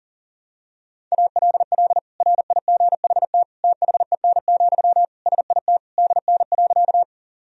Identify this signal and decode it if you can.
Morse code, heard as a keyed tone, about 40 wpm, 700 Hz. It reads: ALL RIGHT THEN, SIT DN.
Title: Sentences from Top 500 Words and Abbreviations 40wpm